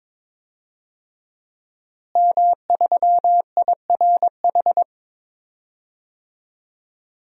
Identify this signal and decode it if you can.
Morse code, heard as a keyed tone, about 22 wpm, 700 Hz. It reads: M3IRH
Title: DXCC-100 Call Signs - 5 Characters - Repeated 2x - 22wpm